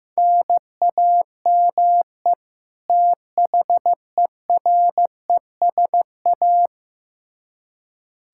Morse code, heard as a keyed tone, about 15 wpm, 700 Hz. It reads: NAME THERESA
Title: K1USN SST QSO Practice 15wpm